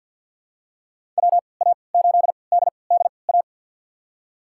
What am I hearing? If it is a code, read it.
Morse code, heard as a keyed tone, about 48 wpm, spaced as for 28 wpm, 700 Hz. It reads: WA8DDA
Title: USA Call Signs - 6 Characters - 3x Character Spacing - 50wpm